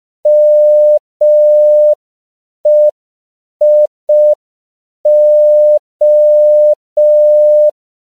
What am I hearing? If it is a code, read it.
Morse code, heard as a keyed tone, about 5 wpm, 600 Hz. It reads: MEIO
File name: CW / 5PPM